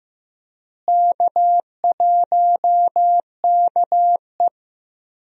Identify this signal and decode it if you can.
Morse code, heard as a keyed tone, about 15 wpm, 700 Hz. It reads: K1KE
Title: USA Call Signs - 4 Characters - Repeated 3x - 15wpm